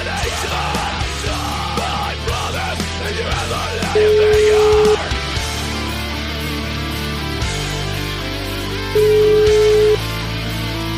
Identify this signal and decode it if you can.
Ringback tone (Europe, 425 Hz), with noise.